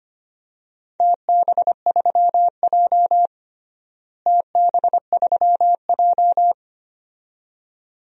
Morse code, heard as a keyed tone, about 25 wpm, 700 Hz. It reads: TB3J TB3J
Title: DXCC-100 Call Signs - 4 Characters - Repeated 2x - 25wpm